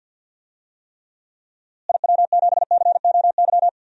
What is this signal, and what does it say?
Morse code, heard as a keyed tone, about 52 wpm, 700 Hz. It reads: IW7COY
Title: DXCC-100 Call Signs - 6 Characters - Repeated 2x - 50wpm